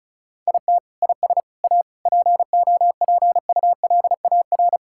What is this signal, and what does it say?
Morse code, heard as a keyed tone, about 36 wpm, 700 Hz. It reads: IT IS A POPULAR
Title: Sentences from Top 1000 Words - Review Part 6 - 35wpm